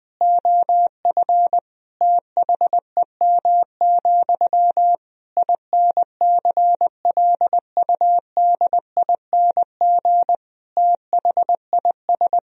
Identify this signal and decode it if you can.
Morse code, heard as a keyed tone, about 20 wpm, 700 Hz. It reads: OF THEM, INCLUDING THIS
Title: Sentences from Top 1000 Words - Review Part 4 - 20wpm